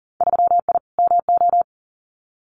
Morse code, heard as a keyed tone, about 40 wpm, 700 Hz. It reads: 3I MO
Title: Winter Field Day Exchanges 40wpm